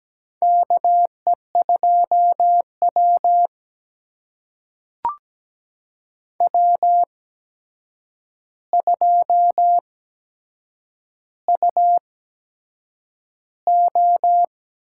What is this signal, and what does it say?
Morse code, heard as a keyed tone, about 17 wpm, spaced as for 7 wpm, 700 Hz. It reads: KE2W W2UO
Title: USA Call Signs - 4 Characters - 8x Character Spacing - 17wpm